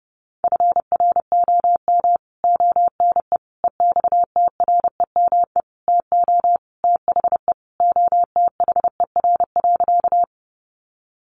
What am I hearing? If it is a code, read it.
Morse code, heard as a keyed tone, about 30 wpm, 700 Hz. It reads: FROM ONE EXTREME TO THE OTHER.
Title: Sentences from Top 1300 Words 30wpm